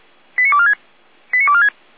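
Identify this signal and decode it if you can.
Five-tone selective call (ZVEI), heard as quick runs of five tones, tones of about 70 ms. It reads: 89246 89246
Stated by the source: ZVEI five-tone selective call received on an SDR, 455 channel.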